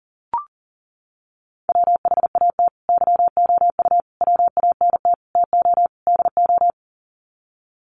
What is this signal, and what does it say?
Morse code, heard as a keyed tone, about 40 wpm, 700 Hz. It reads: WHAT YOU WANT TO DO
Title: Sets of 5 Words Encore 40wpm